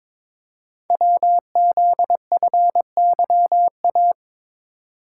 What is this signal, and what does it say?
Morse code, heard as a keyed tone, about 22 wpm, 700 Hz. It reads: WZFYA
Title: Code Groups - 5 Letters - 22wpm